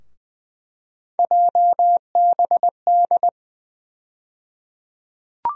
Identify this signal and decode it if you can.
Morse code, heard as a keyed tone, about 20 wpm, 700 Hz. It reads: JBD
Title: USA Call Sign Postfixes - 3 Letters 20wpm